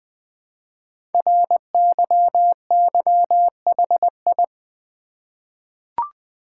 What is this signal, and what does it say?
Morse code, heard as a keyed tone, about 20 wpm, 700 Hz. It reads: RYYHI
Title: Code Groups - 5 Letters - 20wpm